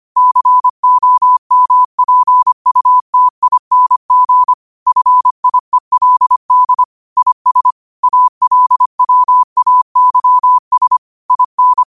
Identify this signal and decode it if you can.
Morse code, heard as a keyed tone, about 25 wpm, 1000 Hz. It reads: COMPUTING FIELD IS ALWAYS IN
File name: qotdfiles / 25wpm